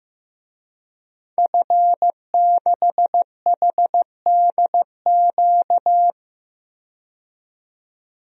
Morse code, heard as a keyed tone, about 15 wpm, 700 Hz. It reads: F6HDQ
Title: DXCC-100 Call Signs - 5 Characters - 15wpm